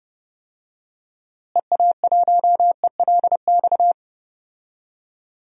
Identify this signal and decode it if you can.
Morse code, heard as a keyed tone, about 30 wpm, 700 Hz. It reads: EA1ELX
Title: DXCC-100 Call Signs - 6 Characters - Repeated 2x - 30wpm